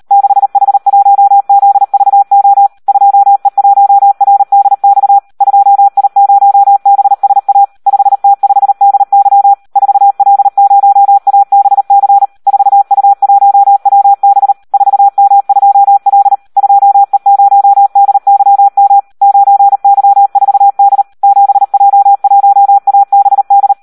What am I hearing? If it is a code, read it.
Morse code, heard as a keyed tone, about 38 wpm, 800 Hz. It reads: 6H1ZVO 2E1RDX 2I0BSA 5T5DY 4L0ABC 4U1WB 4M2L 2E0DYM 9Y4D 7J1ABD
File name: Audio practica CW / SD 1 38